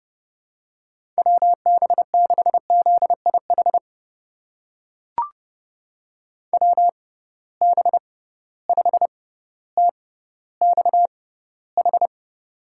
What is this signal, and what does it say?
Morse code, heard as a keyed tone, about 30 wpm, spaced as for 15 wpm, 700 Hz. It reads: WB6ZIH WB5TXH
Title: USA Call Signs - 6 Characters - 6x Character Spacing - 30wpm